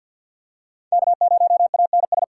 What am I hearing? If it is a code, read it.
Morse code, heard as a keyed tone, about 46 wpm, 700 Hz. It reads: K0ANS
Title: USA Call Signs - 5 Characters - 6x Character Spacing - 50wpm